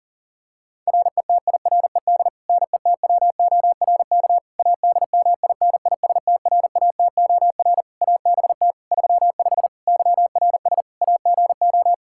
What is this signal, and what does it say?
Morse code, heard as a keyed tone, about 37 wpm, 700 Hz. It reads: RETIRED NETWORK ADMINISTRATOR ABT 35 YRS AGO
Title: Practice QSO - Exchange 3 - More Info - 40wpm